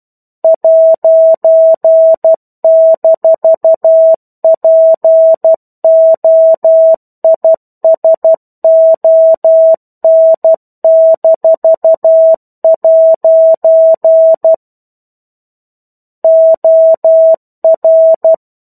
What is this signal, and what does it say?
Morse code, heard as a keyed tone, about 12 wpm, 650 Hz. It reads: '-POISON-' OR
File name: 01-Chapter-01 12 WPM